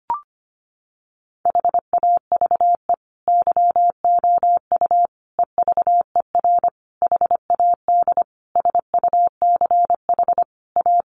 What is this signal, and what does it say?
Morse code, heard as a keyed tone, about 25 wpm, 700 Hz. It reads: HAVE YOU EVER HAD SUCH A
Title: Sentences from Top 1200 Words 25wpm